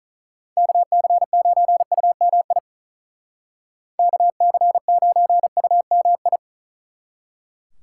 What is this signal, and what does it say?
Morse code, heard as a keyed tone, about 45 wpm, 700 Hz. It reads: KC9UMI KC9UMI